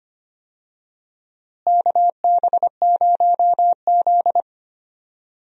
Morse code, heard as a keyed tone, about 25 wpm, 700 Hz. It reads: KB0Z
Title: USA Call Signs - 4 Characters - Repeated 2x - 25wpm